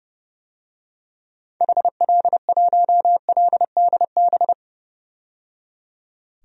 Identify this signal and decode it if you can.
Morse code, heard as a keyed tone, about 31 wpm, 700 Hz. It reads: HL1LDB
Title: DXCC-100 Call Signs - 6 Characters - 30wpm